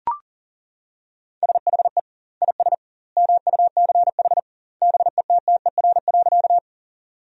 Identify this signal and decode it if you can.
Morse code, heard as a keyed tone, about 37 wpm, 700 Hz. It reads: SHE IS MUCH BETTER.
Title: 2x Word-Spacing - Sentences from Top 400 Words - 40wpm